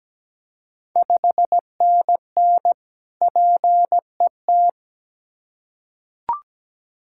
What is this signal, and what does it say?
Morse code, heard as a keyed tone, about 17 wpm, 700 Hz. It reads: 5NN PET
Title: Ontario QSO Party Exchange 17wpm